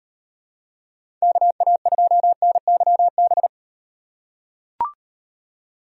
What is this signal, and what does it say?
Morse code, heard as a keyed tone, about 35 wpm, 700 Hz. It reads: KA2NYB